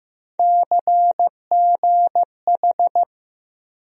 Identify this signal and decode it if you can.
Morse code, heard as a keyed tone, about 15 wpm, 700 Hz. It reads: CGH
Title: USA Call Sign Postfixes - 3 Letters 15wpm